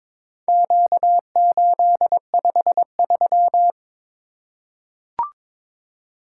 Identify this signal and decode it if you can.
Morse code, heard as a keyed tone, about 23 wpm, 700 Hz. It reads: Q853
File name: practice / 22wpm / Four Letter-Number Combinations - 22wpm